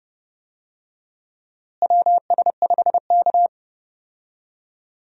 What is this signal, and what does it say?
Morse code, heard as a keyed tone, about 32 wpm, 700 Hz. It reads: WS5K
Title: USA Call Signs - 4 Characters - Repeated 2x - 30wpm